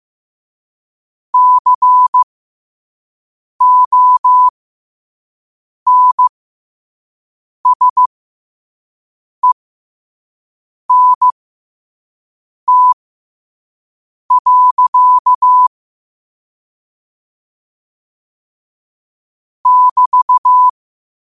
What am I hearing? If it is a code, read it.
Morse code, heard as a keyed tone, about 15 wpm, spaced as for 5 wpm, 1000 Hz. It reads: CONSENT. <BT>